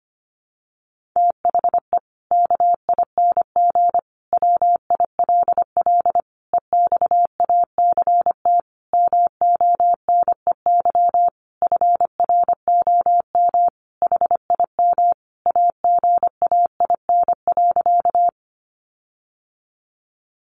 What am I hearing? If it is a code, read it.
Morse code, heard as a keyed tone, about 25 wpm, 700 Hz. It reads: THE KING WILL EXACT MONEY FROM HIM AGAIN.
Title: Sentences from Top 1300 Words 25wpm